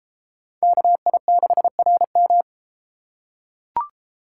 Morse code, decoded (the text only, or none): KI6RM